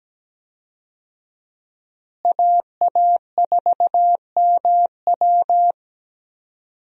Morse code, heard as a keyed tone, about 17 wpm, 700 Hz. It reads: AA4MW
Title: USA Call Signs - 5 Characters - Repeated 2x - 17wpm